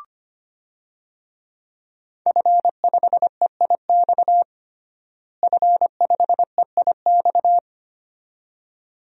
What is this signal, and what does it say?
Morse code, heard as a keyed tone, about 25 wpm, 700 Hz. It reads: F5EIX F5EIX
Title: DXCC-100 Call Signs - 5 Characters - Repeated 2x - 25wpm